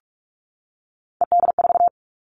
Morse code, heard as a keyed tone, about 44 wpm, 700 Hz. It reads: ED4
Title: All Cut Numbers - 3-Digit 45wpm